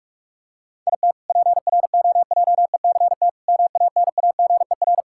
Morse code, heard as a keyed tone, about 46 wpm, 700 Hz. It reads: IT PROJECT MANAGER
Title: K1USN SST QSO Practice 45wpm